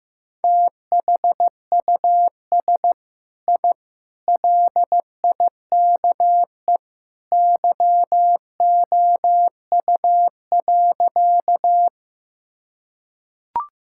Morse code, heard as a keyed tone, about 15 wpm, 700 Hz. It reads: THUS I LIKE YOU.
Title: Sentences from Top 1000 Words - Review Part 5 - 15wpm